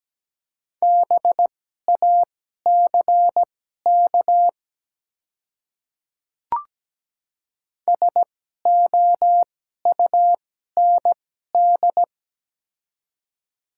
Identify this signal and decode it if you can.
Morse code, heard as a keyed tone, about 17 wpm, spaced as for 12 wpm, 700 Hz. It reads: BACK SOUND
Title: 2x Character Spacing - Top 200 Words - 17wpm